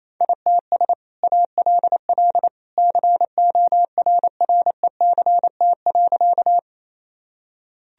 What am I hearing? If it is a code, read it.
Morse code, heard as a keyed tone, about 27 wpm, 700 Hz. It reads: ITS ALL CORRECT.
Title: Sentences from Top 1000 Words - Review Part 2 - 28wpm